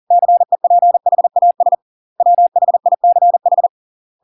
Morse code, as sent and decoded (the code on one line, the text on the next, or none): -.-. .--. .... .- ... / .-- .... .. -.-. ....
CEPHAS WHICH